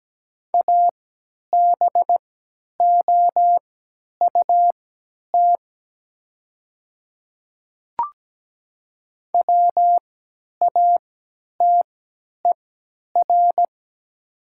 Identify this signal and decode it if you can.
Morse code, heard as a keyed tone, about 17 wpm, spaced as for 10 wpm, 700 Hz. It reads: ABOUT WATER
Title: 3x Character Spacing - Top 100 Words - 17wpm